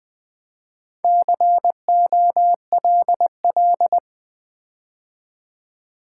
Morse code, heard as a keyed tone, about 20 wpm, 700 Hz. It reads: COLL
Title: Build A Word - 4 Letters At A Time - 10-Letter Words - 20wpm